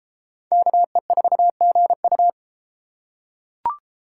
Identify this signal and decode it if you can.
Morse code, heard as a keyed tone, about 30 wpm, 700 Hz. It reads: KE4GU